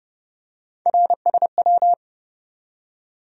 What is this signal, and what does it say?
Morse code, heard as a keyed tone, about 30 wpm, 700 Hz. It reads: RSW
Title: USA Call Sign Postfixes - 3 Letters 30wpm